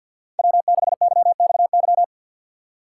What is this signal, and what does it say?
Morse code, heard as a keyed tone, about 50 wpm, 700 Hz. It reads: W6YXY
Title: USA Call Signs - 5 Characters - 4x Character Spacing - 50wpm